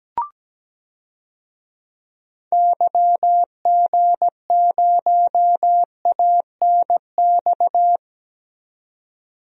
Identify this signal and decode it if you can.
Morse code, heard as a keyed tone, about 17 wpm, 700 Hz. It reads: YG0ANX
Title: DXCC-100 Call Signs - 6 Characters - 17wpm